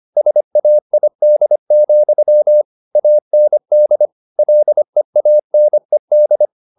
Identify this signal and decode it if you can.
Morse code, heard as a keyed tone, about 25 wpm, 600 Hz. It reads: SAID, AND LEANED